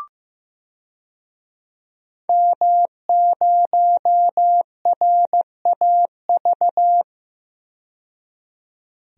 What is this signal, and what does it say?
Morse code, heard as a keyed tone, about 15 wpm, 700 Hz. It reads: M0RAV
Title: DXCC-100 Call Signs - 5 Characters - 15wpm